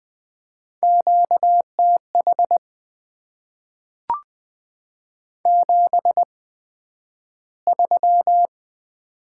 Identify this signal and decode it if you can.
Morse code, heard as a keyed tone, about 20 wpm, spaced as for 11 wpm, 700 Hz. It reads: QTH 73
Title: Top 20 QSO Elements - 8x Character Spacing - 20wpm